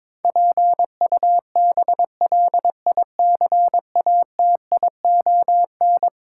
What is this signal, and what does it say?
Morse code, heard as a keyed tone, about 22 wpm, 700 Hz. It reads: PUBLICATION
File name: practice / 22wpm / Eleven Letter Words - 22wpm